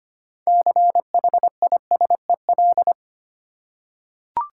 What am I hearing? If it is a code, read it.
Morse code, heard as a keyed tone, about 25 wpm, 700 Hz. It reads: CHISEL
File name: practice / 25wpm / Difficult Words with Hints - 25wpm